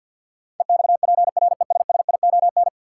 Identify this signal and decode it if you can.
Morse code, heard as a keyed tone, about 52 wpm, 700 Hz. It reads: EXPRESSION
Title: Sentences from Top 1200 Words - New Word Drill - 50wpm